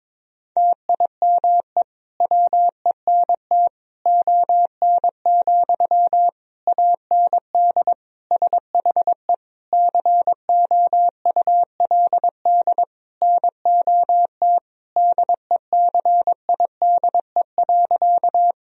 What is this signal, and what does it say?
Morse code, heard as a keyed tone, about 22 wpm, 690 Hz. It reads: TIME WENT ON, AND SHE COULD NOT DECIDE.
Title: Sentences from Top 1000 Words - Review Part 3 - 22wpm